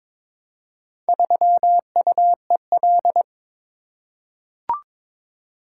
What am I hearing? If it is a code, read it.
Morse code, heard as a keyed tone, about 22 wpm, 700 Hz. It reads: 3UEL
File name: practice / 22wpm / Four Letter-Number Combinations - 22wpm